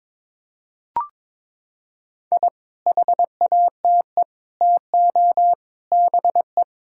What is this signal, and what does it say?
Morse code, heard as a keyed tone, about 22 wpm, 700 Hz. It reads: I HATE TO BE
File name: Sets of 9 Words - 22wpm